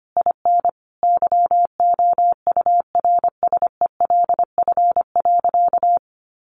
Morse code, heard as a keyed tone, about 25 wpm, 700 Hz. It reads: IN YOURSELF.
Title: Sentences from Top 1500 Words 25wpm